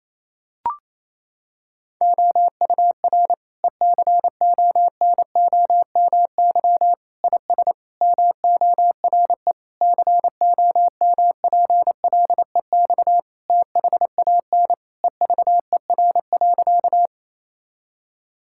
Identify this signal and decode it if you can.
Morse code, heard as a keyed tone, about 28 wpm, 700 Hz. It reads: OUR ECONOMY IS MORE COMPLEX THAN EVER.
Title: Sentences from Top 1000 Words - Review Part 6 - 28wpm